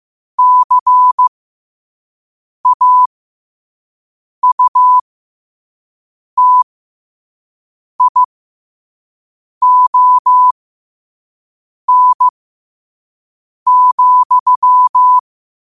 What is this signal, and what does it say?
Morse code, heard as a keyed tone, about 15 wpm, spaced as for 5 wpm, 1000 Hz. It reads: CAUTION,
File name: qotdfiles / 5wpm